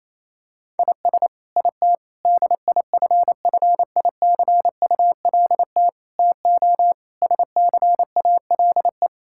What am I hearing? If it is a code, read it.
Morse code, heard as a keyed tone, about 28 wpm, 700 Hz. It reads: IS IT DIFFICULT TO SCALE
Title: Sentences from Top 1000 Words - Review Part 5 - 28wpm